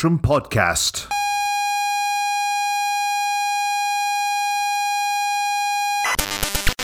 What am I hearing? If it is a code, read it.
Computer data on cassette, with speech.